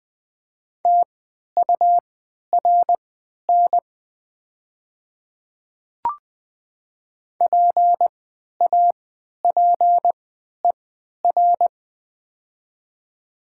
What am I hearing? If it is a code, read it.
Morse code, heard as a keyed tone, about 20 wpm, spaced as for 11 wpm, 700 Hz. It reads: TURN PAPER
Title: 3x Character Spacing - Top 300 Words - 20wpm